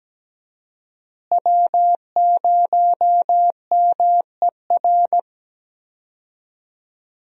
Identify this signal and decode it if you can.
Morse code, heard as a keyed tone, about 17 wpm, 700 Hz. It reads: W0MER
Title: USA Call Signs - 5 Characters - Repeated 3x - 17wpm